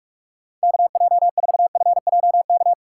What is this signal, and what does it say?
Morse code, heard as a keyed tone, about 46 wpm, 700 Hz. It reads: KJ4FJK
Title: USA Call Signs - 4x Character Spacing - 45wpm